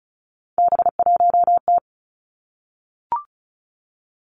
Morse code, heard as a keyed tone, about 36 wpm, 700 Hz. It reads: B1T
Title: Three Challenging Letter-Number Combinations 35wpm